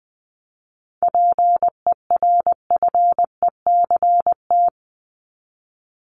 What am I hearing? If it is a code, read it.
Morse code, heard as a keyed tone, about 20 wpm, 700 Hz. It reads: PERFECT
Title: LY Suffix - Top 100 Words with Base Words 20wpm